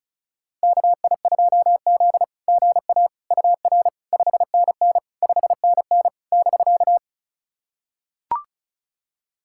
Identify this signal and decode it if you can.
Morse code, heard as a keyed tone, about 35 wpm, 690 Hz. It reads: KI2Z GA UR 5NN 5NN <BK>
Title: SOTA QSO Practice 35wpm